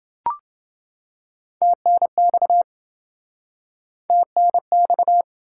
Morse code, heard as a keyed tone, about 30 wpm, 700 Hz. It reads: TNX TNX